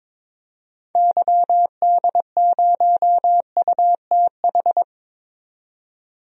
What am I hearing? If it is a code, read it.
Morse code, heard as a keyed tone, about 22 wpm, 700 Hz. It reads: YD0UTH
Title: DXCC-100 Call Signs - 6 Characters - 22wpm